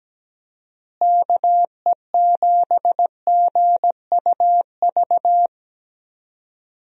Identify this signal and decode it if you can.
Morse code, heard as a keyed tone, about 17 wpm, 700 Hz. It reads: KE7GUV
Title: USA Call Signs - 6 Characters - Repeated 2x - 17wpm